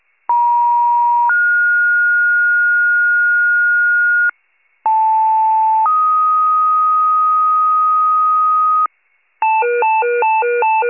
Two-tone page, heard as a one-second tone, then a three-second tone, about 950 then 1430 Hz, about 880 then 1250 Hz, with noise. It is heard from 0.3 to 8.9 s.